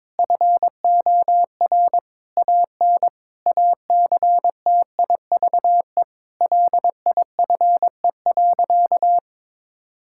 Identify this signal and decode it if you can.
Morse code, heard as a keyed tone, about 22 wpm, 700 Hz. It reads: FOR AN ACTIVE LIFE.
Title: Sentences from Top 1000 Words - Review Part 6 - 22wpm